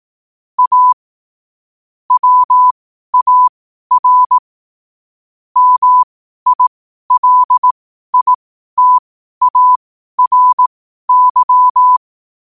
Morse code, heard as a keyed tone, about 18 wpm, spaced as for 12 wpm, 1000 Hz. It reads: A WAR MILITARY
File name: 12wpm